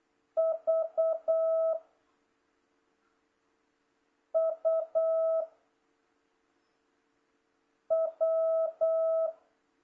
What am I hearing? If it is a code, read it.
Morse code, heard as a keyed tone, about 8 wpm, 660 Hz. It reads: V U W